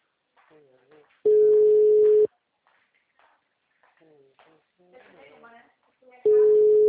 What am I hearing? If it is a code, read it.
Ringback tone (Europe, 425 Hz), with speech.